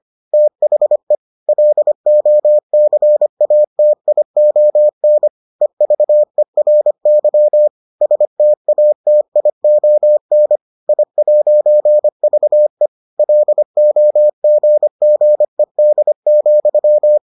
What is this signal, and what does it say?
Morse code, heard as a keyed tone, about 25 wpm, 600 Hz. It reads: THE LOCATION EVERY STATION I'VE LOGGED,